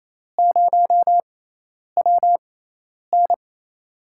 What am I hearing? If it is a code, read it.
Morse code, heard as a keyed tone, about 28 wpm, spaced as for 10 wpm, 700 Hz. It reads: 0WN
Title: USA Call Signs - 6x Character Spacing - 28wpm